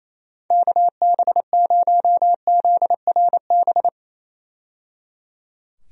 Morse code, heard as a keyed tone, about 28 wpm, 700 Hz. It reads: KB0ZRB
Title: USA Call Signs - 6 Characters - 28wpm